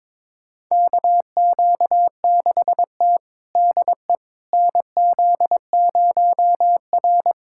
Morse code, heard as a keyed tone, about 22 wpm, 700 Hz. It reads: KQ6T DE NZ0R
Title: USA Callsign DE Callsign - 4 Characters - 22wpm